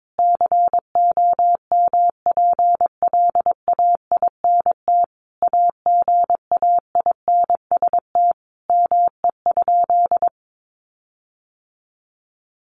Morse code, heard as a keyed tone, about 22 wpm, 700 Hz. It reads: COMPLAINT AGAINST ME?
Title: Sentences from Top 1400 Words 22wpm